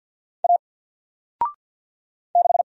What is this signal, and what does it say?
Morse code, heard as a keyed tone, about 50 wpm, 700 Hz. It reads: A 6
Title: Single Letter-Number 50wpm